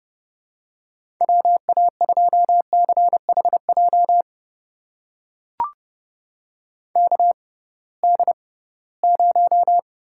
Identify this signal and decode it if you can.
Morse code, heard as a keyed tone, about 29 wpm, spaced as for 19 wpm, 700 Hz. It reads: WA2CHJ KD0
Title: USA Call Signs - 6 Characters - 6x Character Spacing - 30wpm